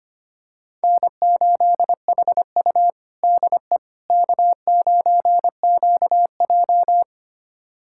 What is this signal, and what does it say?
Morse code, heard as a keyed tone, about 25 wpm, 700 Hz. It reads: N8HU DE K9QJ
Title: USA Callsign DE Callsign - 4 Characters - 25wpm